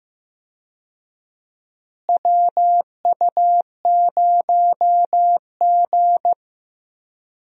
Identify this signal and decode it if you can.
Morse code, heard as a keyed tone, about 15 wpm, 700 Hz. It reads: WU0G
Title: USA Call Signs - 4 Characters - Repeated 2x - 15wpm